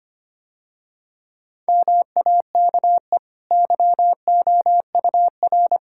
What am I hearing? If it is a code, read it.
Morse code, heard as a keyed tone, about 25 wpm, 690 Hz. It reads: MAKE YOUR
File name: Easy Sentences with Hints - 25wpm